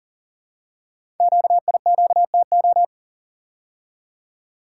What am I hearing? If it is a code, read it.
Morse code, heard as a keyed tone, about 44 wpm, 700 Hz. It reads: QIQTO
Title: Code Groups - 5 Letters - 40wpm